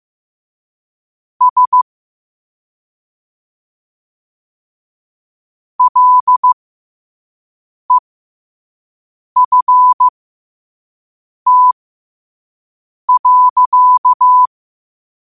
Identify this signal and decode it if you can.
Morse code, heard as a keyed tone, about 15 wpm, spaced as for 5 wpm, 1000 Hz. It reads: S LEFT.
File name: qotdfiles / 5wpm